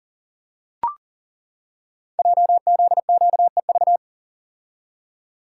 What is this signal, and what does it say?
Morse code, heard as a keyed tone, about 39 wpm, 700 Hz. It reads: JZQEV
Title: Code Groups - 5 Letters - 40wpm